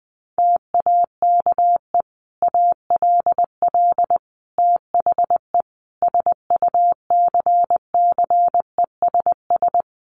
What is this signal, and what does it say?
Morse code, heard as a keyed tone, about 19 wpm, 700 Hz. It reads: TAKE ALL THE SUCCESS
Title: Sentences from Top 900 Words 20wpm